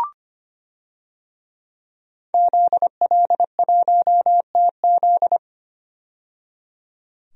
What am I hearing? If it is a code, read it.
Morse code, heard as a keyed tone, about 25 wpm, 700 Hz. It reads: ZL1TZ
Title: DXCC-100 Call Signs - 5 Characters - 25wpm